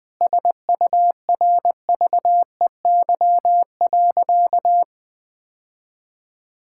Morse code, heard as a keyed tone, about 20 wpm, 700 Hz. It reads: SURVEY.
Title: Sentences with 7-Word Phrases - 20wpm